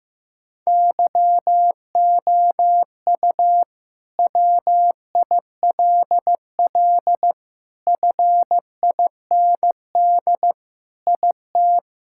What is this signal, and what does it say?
Morse code, heard as a keyed tone, about 15 wpm, 700 Hz. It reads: YOU WILL FIND IT
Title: Sentences from Top 1600 Words 15wpm